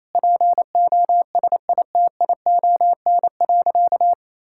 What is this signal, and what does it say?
Morse code, heard as a keyed tone, about 28 wpm, 700 Hz. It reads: POSITION.
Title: Sentences from Top 1000 Words - Review Part 5 - 28wpm